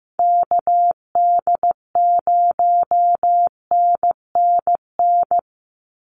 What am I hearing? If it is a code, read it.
Morse code, heard as a keyed tone, about 15 wpm, 700 Hz. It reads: KD0NNN
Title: USA Call Signs 15wpm